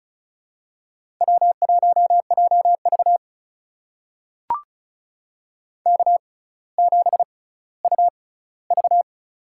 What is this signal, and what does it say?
Morse code, heard as a keyed tone, about 36 wpm, spaced as for 18 wpm, 700 Hz. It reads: W1JV K7UV
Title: USA Call Signs - 4 Characters - 6x Character Spacing - 35wpm